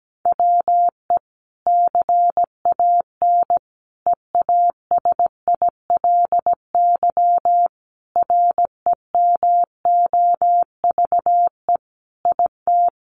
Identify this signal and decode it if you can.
Morse code, heard as a keyed tone, about 17 wpm, 700 Hz. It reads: WE CAN EASILY REMOVE IT
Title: Sentences from Top 1200 Words 17wpm